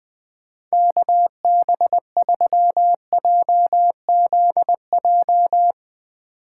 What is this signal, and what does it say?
Morse code, heard as a keyed tone, about 21 wpm, 700 Hz. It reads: KB3JZJ